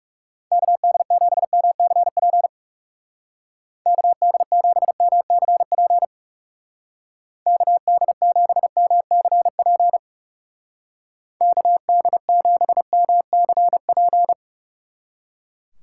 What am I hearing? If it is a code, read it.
Morse code, heard as a keyed tone, about 36 wpm, 700 Hz. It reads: KD7MCP KD7MCP KD7MCP KD7MCP
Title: Speed Racing - USA 2x3 Call Signs 30wpm